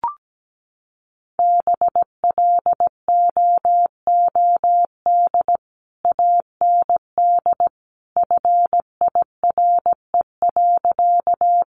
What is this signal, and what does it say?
Morse code, heard as a keyed tone, about 17 wpm, 700 Hz. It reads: BLOOD AND FIRE.
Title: Sentences from Top 1100 Words 17wpm